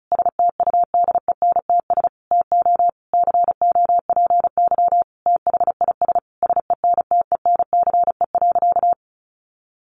Morse code, heard as a keyed tone, about 34 wpm, 700 Hz. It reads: STUDENTS TO COPY THIS SENTENCE.
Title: Additional Easy Phrases 35wpm